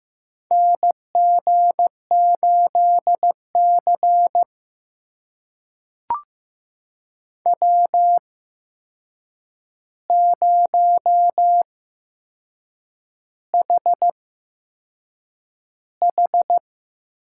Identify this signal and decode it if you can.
Morse code, heard as a keyed tone, about 15 wpm, spaced as for 6 wpm, 700 Hz. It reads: NG8C W0HH
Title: USA Call Signs - 4 Characters - 8x Character Spacing - 15wpm